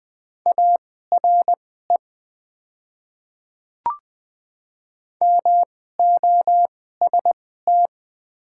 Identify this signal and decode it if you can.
Morse code, heard as a keyed tone, about 20 wpm, spaced as for 15 wpm, 700 Hz. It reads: ARE MOST
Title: 2x Character Spacing - Top 300 Words - 20wpm